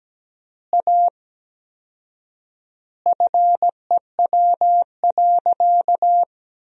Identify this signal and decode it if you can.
Morse code, heard as a keyed tone, about 17 wpm, 700 Hz. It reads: A FEW.
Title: 4x Word-Spacing - Sentences from Top 300 Words - 17wpm